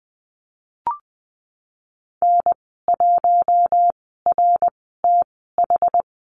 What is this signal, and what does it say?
Morse code, heard as a keyed tone, about 20 wpm, spaced as for 15 wpm, 700 Hz. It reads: N1RTH
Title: 062 Lesson 11g - Letter h - Call Signs 20wpm